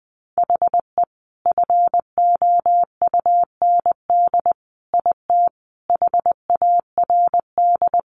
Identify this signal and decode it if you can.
Morse code, heard as a keyed tone, about 20 wpm, 700 Hz. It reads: HE FOUND IT HARD
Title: Sentences from Top 1400 Words 20wpm